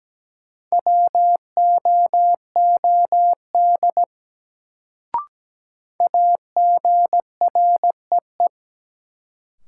Morse code, heard as a keyed tone, about 17 wpm, 700 Hz. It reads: WOOD AGREE
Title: Sentences from Top 1100 Words - New Word Drill Alt - 17wpm